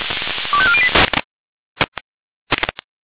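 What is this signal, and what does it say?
Five-tone selective call (ZVEI), heard as quick runs of five tones, tones of about 70 ms. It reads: 25408
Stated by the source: ZVEI five-tone call received via SDR.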